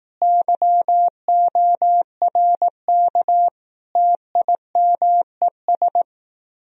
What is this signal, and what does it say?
Morse code, heard as a keyed tone, about 18 wpm, 700 Hz. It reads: YORK TIMES
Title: Speed-Racing - Sets of 5 Words 15wpm